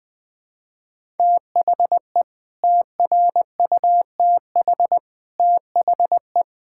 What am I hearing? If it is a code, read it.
Morse code, heard as a keyed tone, about 20 wpm, 700 Hz. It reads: THE TRUTH THE
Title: Sets of 9 Words - 20wpm